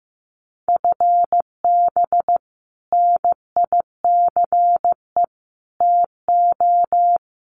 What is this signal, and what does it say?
Morse code, heard as a keyed tone, about 15 wpm, 700 Hz. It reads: FB NICE TO
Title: Common QSO Phrases 15wpm